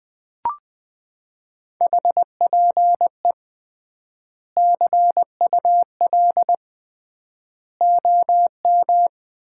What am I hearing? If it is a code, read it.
Morse code, heard as a keyed tone, about 20 wpm, 700 Hz. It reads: HPE CUL OM
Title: Common QSO Phrases - 3x Word-Spacing - 20wpm